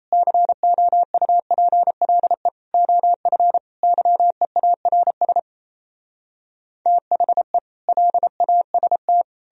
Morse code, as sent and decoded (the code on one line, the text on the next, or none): -.-. --- ..- .--. .-.. / --- ..-. / -.-- . .- .-. ... / - .... / .-.. .- ... -
COUPLE OF YEARS THE LAST